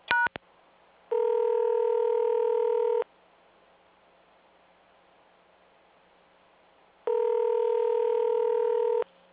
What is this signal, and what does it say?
Ringback tone (North America), 440 + 480 Hz.